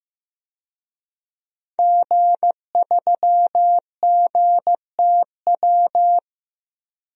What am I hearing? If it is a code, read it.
Morse code, heard as a keyed tone, about 15 wpm, 700 Hz. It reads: G3GTW
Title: DXCC-100 Call Signs - 5 Characters - 15wpm